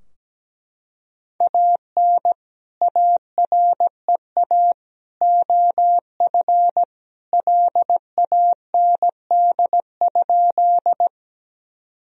Morse code, heard as a keyed tone, about 17 wpm, 700 Hz. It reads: AN AREA OF LAND?
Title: Sentences from Top 1000 Words - Review Part 2 - 17wpm